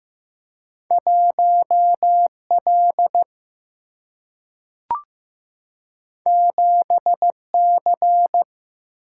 Morse code, heard as a keyed tone, about 15 wpm, 700 Hz. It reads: 1L 7C
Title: Single-Letter Single-Number 15wpm